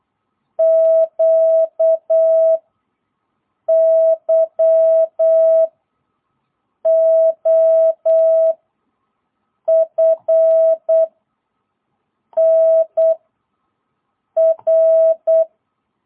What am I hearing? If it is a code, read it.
Morse code, heard as a keyed tone, about 8 wpm, 660 Hz. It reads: Q Y O F N R
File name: jh8cbh / 2022cw